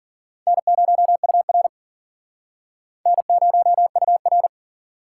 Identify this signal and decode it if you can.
Morse code, heard as a keyed tone, about 51 wpm, 700 Hz. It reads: N0UR N0UR